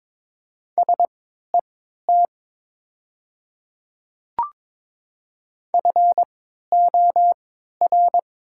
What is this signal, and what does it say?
Morse code, heard as a keyed tone, about 22 wpm, spaced as for 12 wpm, 700 Hz. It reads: SET FOR
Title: 3x Character Spacing - Top 200 Words - 22wpm